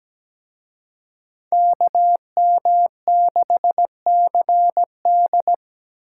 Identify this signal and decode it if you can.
Morse code, heard as a keyed tone, about 20 wpm, 700 Hz. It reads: KM6CD
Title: Speed Racing - USA 2x2 Call Signs 17wpm